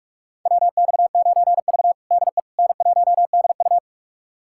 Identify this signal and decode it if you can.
Morse code, heard as a keyed tone, about 46 wpm, 700 Hz. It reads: WX9V DE N1DU
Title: USA Callsign DE Callsign - 4 Characters - 45wpm